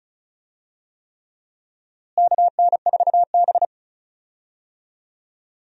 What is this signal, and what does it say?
Morse code, heard as a keyed tone, about 36 wpm, 700 Hz. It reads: KN4B
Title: USA Call Signs - 4 Characters - Repeated 3x - 35wpm